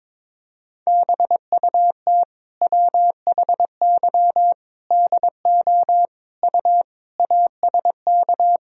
Morse code, heard as a keyed tone, about 22 wpm, 700 Hz. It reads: BUT WHY DO U ASK
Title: Sentences from Top 200 Words and Abbreviations 22wpm